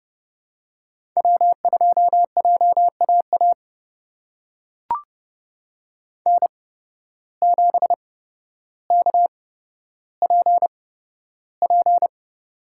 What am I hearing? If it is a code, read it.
Morse code, heard as a keyed tone, about 30 wpm, spaced as for 13 wpm, 700 Hz. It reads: W2JAA N7KPP
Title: USA Call Signs - 8x Character Spacing - 30wpm